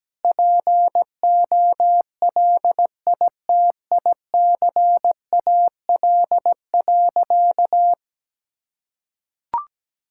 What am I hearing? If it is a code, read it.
Morse code, heard as a keyed tone, about 17 wpm, 700 Hz. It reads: POLITICAL.
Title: Sentences from Top 1600 Words 17wpm